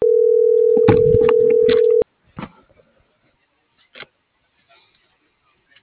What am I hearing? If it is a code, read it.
Ringback tone (North America), 440 + 480 Hz.